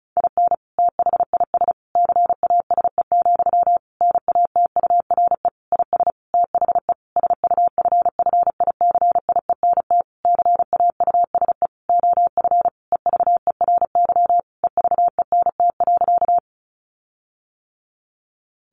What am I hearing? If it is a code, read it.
Morse code, heard as a keyed tone, about 34 wpm, 700 Hz. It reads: IN THIS CASE, NATURE IS THE SUFFICIENT CAUSE OF EVERY EVENT.
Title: Sentences from Top 1400 Words 35wpm